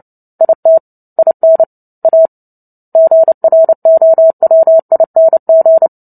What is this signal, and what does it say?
Morse code, heard as a keyed tone, about 30 wpm, 650 Hz. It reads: IT IN A GROWING
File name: Book Two / 03-Chapter 3 - 29 WPM